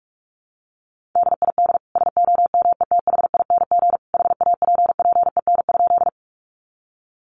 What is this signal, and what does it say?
Morse code, heard as a keyed tone, about 44 wpm, 700 Hz. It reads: DID SOMETHING HAPPEN?
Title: Sentences from Top 600 Words 45wpm